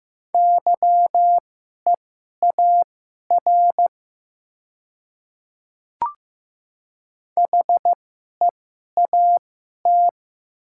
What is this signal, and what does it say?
Morse code, heard as a keyed tone, about 15 wpm, spaced as for 11 wpm, 700 Hz. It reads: YEAR HEAT
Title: 2x Character Spacing - Top 500 Words - 15wpm